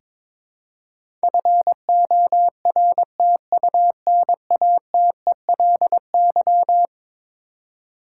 Morse code, heard as a keyed tone, about 22 wpm, 700 Hz. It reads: FORTUNATELY
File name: practice / 22wpm / Eleven Letter Words - 22wpm